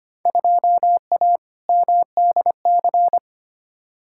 Morse code, heard as a keyed tone, about 25 wpm, 700 Hz. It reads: 2A MDC
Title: Field Day Exchanges - 25wpm